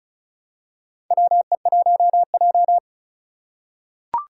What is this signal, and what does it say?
Morse code, heard as a keyed tone, about 34 wpm, 700 Hz. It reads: WE1J
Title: USA Call Signs - 4 Characters - 3x Character Spacing - 35wpm